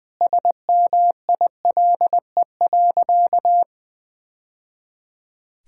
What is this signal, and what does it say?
Morse code, heard as a keyed tone, about 20 wpm, 700 Hz. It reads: SMILE.
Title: Sentences with 2-Word Phrases - 20wpm